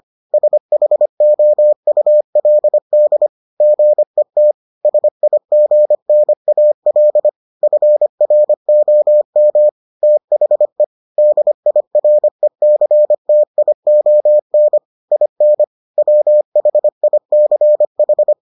Morse code, heard as a keyed tone, about 25 wpm, 600 Hz. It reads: SHOULD GET SIGNAL FROM THE DIRECTION IN WHICH